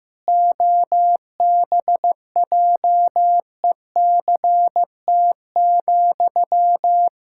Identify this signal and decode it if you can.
Morse code, heard as a keyed tone, about 15 wpm, 700 Hz. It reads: OBJECT,
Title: Sentences from Top 1000 Words - Review Part 6 - 15wpm